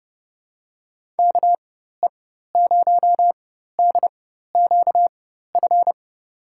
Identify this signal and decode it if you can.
Morse code, heard as a keyed tone, about 30 wpm, spaced as for 14 wpm, 700 Hz. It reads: KE0DQF
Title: USA Call Signs - 6 Characters - 4x Character Spacing - 30wpm